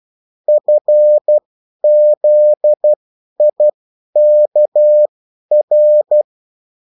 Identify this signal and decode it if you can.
Morse code, heard as a keyed tone, about 12 wpm, spaced as for 10 wpm, 600 Hz. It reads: FZIKR
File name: morse mp3 / revision9-vitesse 10 mots